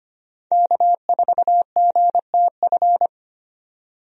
Morse code, heard as a keyed tone, about 25 wpm, 700 Hz. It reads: K4GTF
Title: USA Call Signs - 5 Characters - 25wpm